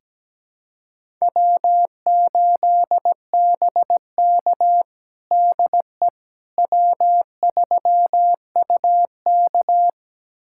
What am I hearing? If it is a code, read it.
Morse code, heard as a keyed tone, about 17 wpm, 700 Hz. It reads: W8BK DE W3UK
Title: USA Callsign DE Callsign - 4 Characters - 17wpm